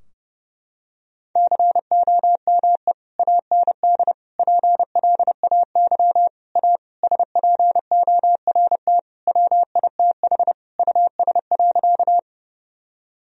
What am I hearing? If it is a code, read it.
Morse code, heard as a keyed tone, about 30 wpm, 700 Hz. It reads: COME AND PLAY A SPORT WITH US.
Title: Sentences from Top 1000 Words - Review Part 4 - 30wpm